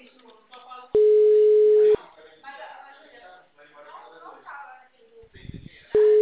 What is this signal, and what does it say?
Ringback tone (Europe, 425 Hz), with speech.